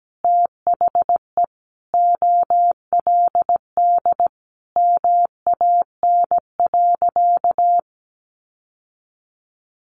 Morse code, heard as a keyed tone, about 17 wpm, 700 Hz. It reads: THE OLD MAN.
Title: Sentences from Top 1100 Words 17wpm